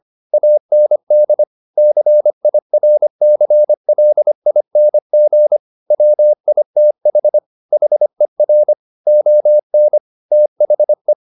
Morse code, heard as a keyed tone, about 25 wpm, 600 Hz. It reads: AND CIRCLING WITH HER ON THE
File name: the old man and the sea part 1 25wpm